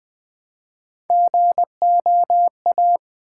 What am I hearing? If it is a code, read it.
Morse code, heard as a keyed tone, about 20 wpm, 700 Hz. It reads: GOA
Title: Build A Word - 1 Letter At A Time - 4-Letter Words - 20wpm